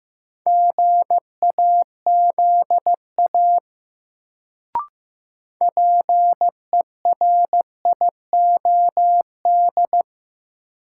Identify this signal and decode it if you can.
Morse code, heard as a keyed tone, about 15 wpm, 700 Hz. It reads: GAZA PERIOD
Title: Sets of 5 Words - New Word Drill - 15wpm